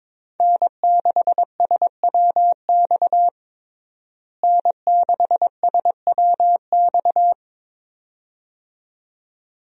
Morse code, heard as a keyed tone, about 22 wpm, 700 Hz. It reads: N6SWX N6SWX
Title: USA Call Signs - 5 Characters - Repeated 2x - 22wpm